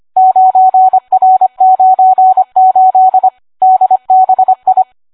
Morse code, heard as a keyed tone, about 25 wpm, 750 Hz. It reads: 9R98 DBI